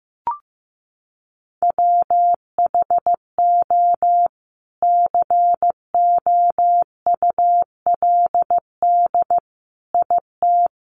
Morse code, heard as a keyed tone, about 15 wpm, 700 Hz. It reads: WHO COULD IT
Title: Sentences from Top 1300 Words 15wpm